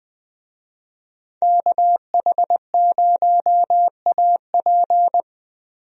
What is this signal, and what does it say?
Morse code, heard as a keyed tone, about 20 wpm, 700 Hz. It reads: KH0AP
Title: USA Call Signs - 5 Characters - Repeated 3x - 20wpm